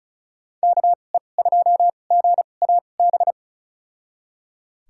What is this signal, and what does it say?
Morse code, heard as a keyed tone, about 36 wpm, spaced as for 25 wpm, 700 Hz. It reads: KE2GAB
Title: USA Call Signs - 6 Characters - 2x Character Spacing - 35wpm